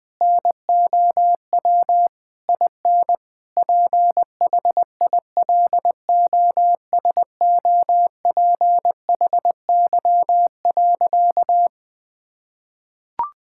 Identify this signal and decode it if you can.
Morse code, heard as a keyed tone, about 21 wpm, 700 Hz. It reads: NOW IN PHILOSOPHY.
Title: Sentences from Top 1000 Words - Review Part 6 - 20wpm